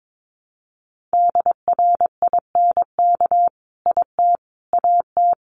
Morse code, heard as a keyed tone, about 22 wpm, 700 Hz. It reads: DRINK IT AT
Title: Sentences from Top 1500 Words 22wpm